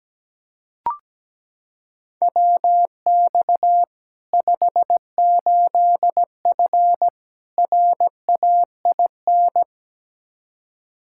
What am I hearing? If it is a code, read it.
Morse code, heard as a keyed tone, about 17 wpm, 700 Hz. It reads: WX 58F RAIN
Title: Practice QSO - Exchange 2 - Info - 17wpm